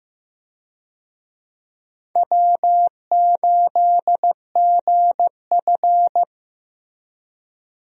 Morse code, heard as a keyed tone, about 15 wpm, 700 Hz. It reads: W8GF
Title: USA Call Signs - 4 Characters - Repeated 2x - 15wpm